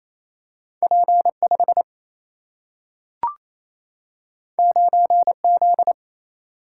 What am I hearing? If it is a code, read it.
Morse code, heard as a keyed tone, about 28 wpm, 700 Hz. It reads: P5 9Z
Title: Single-Letter Single-Number 28wpm